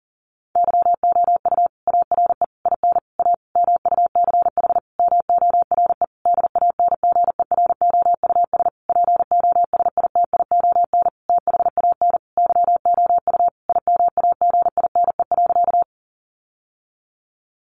Morse code, heard as a keyed tone, about 40 wpm, 700 Hz. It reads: YOU ARE IN A MUCH MORE DANGEROUS POSITION THAN YOU IMAGINE.
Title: Sentences from Top 1400 Words 40wpm